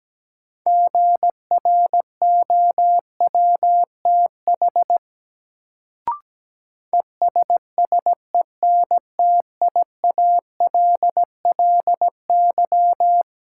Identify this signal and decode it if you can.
Morse code, heard as a keyed tone, about 17 wpm, 700 Hz. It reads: GROWTH ESSENTIALLY
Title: Sentences from Top 1300 Words - New Word Drill Alt - 17wpm